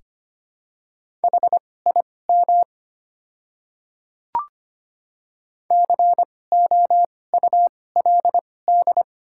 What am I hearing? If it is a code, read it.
Morse code, heard as a keyed tone, about 25 wpm, spaced as for 18 wpm, 700 Hz. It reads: HIM COULD
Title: 2x Character Spacing - Top 300 Words - 25wpm